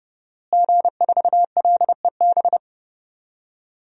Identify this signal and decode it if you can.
Morse code, heard as a keyed tone, about 32 wpm, 700 Hz. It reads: G4LEB
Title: DXCC-100 Call Signs - 5 Characters - Repeated 3x - 30wpm